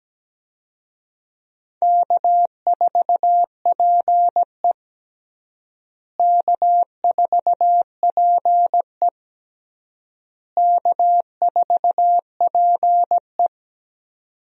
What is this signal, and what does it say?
Morse code, heard as a keyed tone, about 17 wpm, 700 Hz. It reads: K4PE K4PE K4PE
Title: USA Call Signs - 4 Characters - Repeated 3x - 17wpm